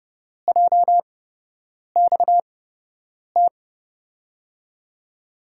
Morse code, heard as a keyed tone, about 31 wpm, spaced as for 8 wpm, 700 Hz. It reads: JXT
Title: USA Call Signs - 8x Character Spacing - 30wpm